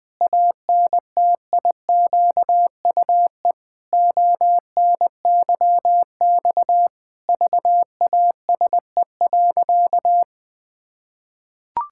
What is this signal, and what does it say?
Morse code, heard as a keyed tone, about 21 wpm, 700 Hz. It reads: ANTIQUE ONYX VASE.
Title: Pangram Challenge v2 - 20wpm